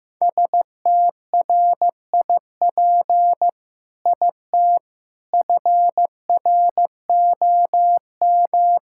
Morse code, heard as a keyed tone, about 15 wpm, 700 Hz. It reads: STRIP IT FROM
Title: Sentences from Top 1600 Words 15wpm